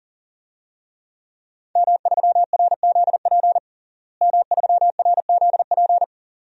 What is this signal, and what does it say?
Morse code, heard as a keyed tone, about 40 wpm, 700 Hz. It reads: M3RZP M3RZP
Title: DXCC-100 Call Signs - 5 Characters - Repeated 2x - 40wpm